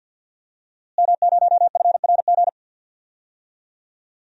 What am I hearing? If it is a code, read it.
Morse code, heard as a keyed tone, about 47 wpm, 700 Hz. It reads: M0FRG
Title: DXCC-100 Call Signs - 5 Characters - Repeated 2x - 50wpm